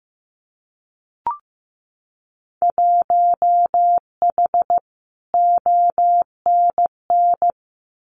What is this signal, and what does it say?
Morse code, heard as a keyed tone, about 15 wpm, 700 Hz. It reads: 1H ONN
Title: Winter Field Day Exchanges 15wpm